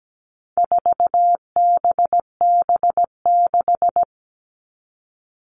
Morse code, heard as a keyed tone, about 17 wpm, 700 Hz. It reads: 4BB6
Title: All Cut Numbers - 4-Digit 17wpm